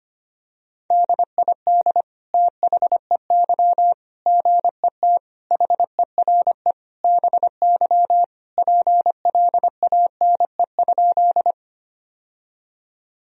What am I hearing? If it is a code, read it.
Morse code, heard as a keyed tone, about 25 wpm, 700 Hz. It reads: DID THEY GET HERE BY PLANE?
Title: Sentences from Top 1000 Words - Review Part 3 - 25wpm